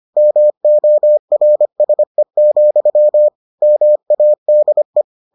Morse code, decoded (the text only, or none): MORSE, MADE